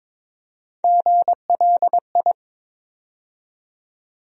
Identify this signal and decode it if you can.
Morse code, heard as a keyed tone, about 22 wpm, 700 Hz. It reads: GLI
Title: USA Call Sign Postfixes - 3 Letters 22wpm